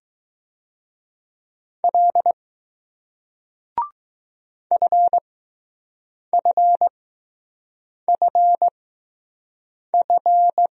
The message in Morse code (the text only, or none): L FFFF